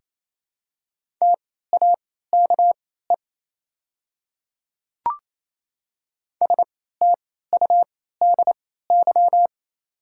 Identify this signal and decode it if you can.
Morse code, heard as a keyed tone, about 28 wpm, spaced as for 16 wpm, 700 Hz. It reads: TAKE STUDY
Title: 3x Character Spacing - Top 300 Words - 28wpm